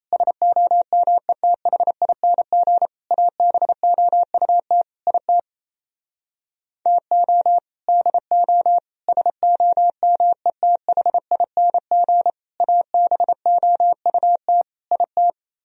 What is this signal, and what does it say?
Morse code, heard as a keyed tone, about 32 wpm, 700 Hz. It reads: SOMETHING ABOUT IT TO DO SOMETHING ABOUT IT
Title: Speed-Racing - Sets of 5 Words 28wpm